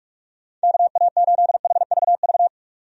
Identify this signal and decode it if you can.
Morse code, heard as a keyed tone, about 46 wpm, 700 Hz. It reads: KA8HVV
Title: USA Call Signs - 6 Characters - 45wpm